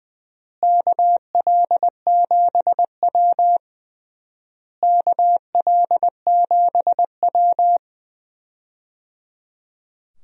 Morse code, heard as a keyed tone, about 20 wpm, 700 Hz. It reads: KL7W KL7W
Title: DXCC-100 Call Signs - 4 Characters - Repeated 3x - 20wpm